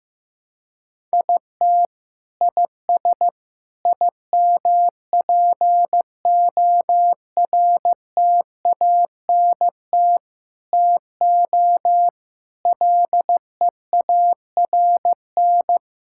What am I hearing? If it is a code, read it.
Morse code, heard as a keyed tone, about 16 wpm, 700 Hz. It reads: IT IS IMPORTANT TO LEARN
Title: Sentences from Top 1000 Words - Review Part 6 - 15wpm